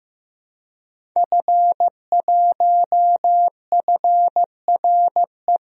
Morse code, heard as a keyed tone, about 15 wpm, 700 Hz. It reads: F1FRE
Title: DXCC-100 Call Signs - 5 Characters - Repeated 3x - 15wpm